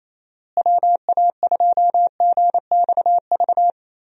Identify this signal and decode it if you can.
Morse code, heard as a keyed tone, about 27 wpm, 700 Hz. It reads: WA2GXV
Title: USA Call Signs - 6 Characters - Repeated 2x - 28wpm